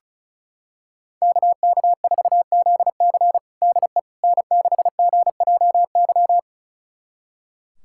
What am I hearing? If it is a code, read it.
Morse code, heard as a keyed tone, about 34 wpm, 700 Hz. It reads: KK4ZC DE N6GJY
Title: USA Callsign DE Callsign - 5 Characters - 35wpm